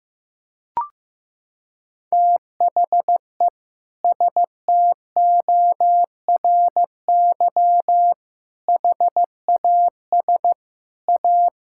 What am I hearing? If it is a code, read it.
Morse code, heard as a keyed tone, about 16 wpm, 700 Hz. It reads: THE STORY HAS A
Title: Sentences from Top 1000 Words - Review Part 6 - 15wpm